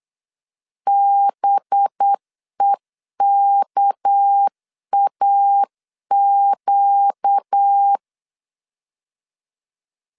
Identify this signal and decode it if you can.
Morse code, heard as a keyed tone, about 8 wpm, 800 Hz. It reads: BEKAQ
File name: CW / DET / 1 6 1500C S C 8wpm 5219